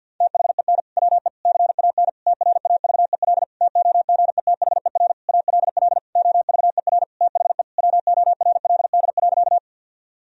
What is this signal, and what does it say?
Morse code, heard as a keyed tone, about 51 wpm, 700 Hz. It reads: THEN WE CAN TRAVEL TOGETHER ALL OVER THE WORLD.
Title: Sentences from Top 1000 Words - Review Part 2 - 50wpm